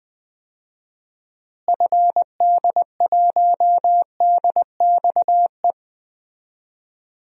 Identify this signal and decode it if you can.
Morse code, heard as a keyed tone, about 20 wpm, 700 Hz. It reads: FD1DXE
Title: DXCC-100 Call Signs - 6 Characters - Repeated 3x - 20wpm